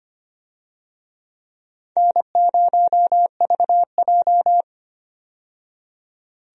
Morse code, heard as a keyed tone, about 25 wpm, 700 Hz. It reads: N0VJ
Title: USA Call Signs - 4 Characters - 25wpm